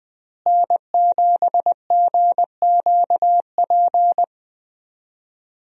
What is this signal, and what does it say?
Morse code, heard as a keyed tone, about 20 wpm, 700 Hz. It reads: N7GQP
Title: USA Call Signs - 5 Characters - Repeated 3x - 20wpm